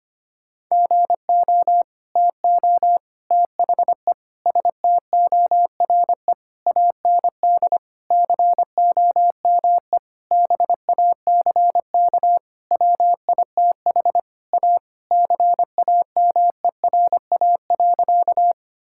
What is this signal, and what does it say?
Morse code, heard as a keyed tone, about 25 wpm, 700 Hz. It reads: GO TO THE STORE AND COME BACK WITH A CAMERA.
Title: Sentences from Top 1000 Words - Review Part 5 - 25wpm